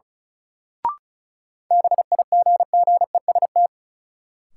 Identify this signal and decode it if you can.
Morse code, heard as a keyed tone, about 36 wpm, 700 Hz. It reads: BIGGEST
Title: Sets of 4 Words - New Word Drill - 35wpm